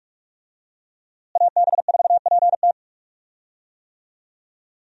Morse code, heard as a keyed tone, about 41 wpm, 700 Hz. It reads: AB4PT
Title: USA Call Signs - 5 Characters - Repeated 3x - 45wpm